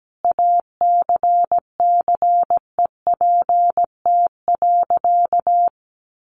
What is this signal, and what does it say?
Morse code, heard as a keyed tone, about 17 wpm, 700 Hz. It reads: ACCEPT.
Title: Sentences from Top 1300 Words 17wpm